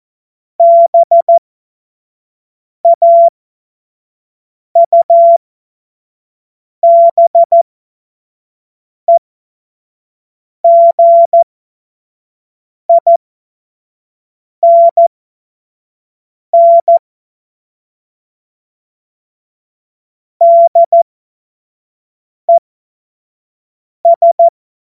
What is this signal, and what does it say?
Morse code, heard as a keyed tone, about 14 wpm, spaced as for 5 wpm, 680 Hz. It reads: BAUBEGINN DES
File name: Morsekurs / Witze / Witz 07.November 2023 25 F70